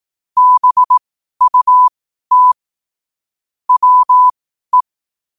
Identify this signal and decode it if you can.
Morse code, heard as a keyed tone, about 18 wpm, spaced as for 12 wpm, 1000 Hz. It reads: BUT WE
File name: qotdfiles / 12wpm